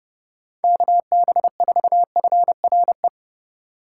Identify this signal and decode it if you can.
Morse code, heard as a keyed tone, about 30 wpm, 700 Hz. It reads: KB4FRE
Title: USA Call Signs - 6 Characters - 4x Character Spacing - 30wpm